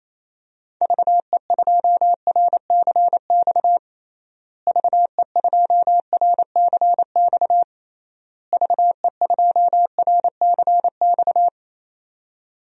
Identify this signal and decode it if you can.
Morse code, heard as a keyed tone, about 28 wpm, 700 Hz. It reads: VE2RCX VE2RCX VE2RCX
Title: DXCC-100 Call Signs - 6 Characters - Repeated 3x - 28wpm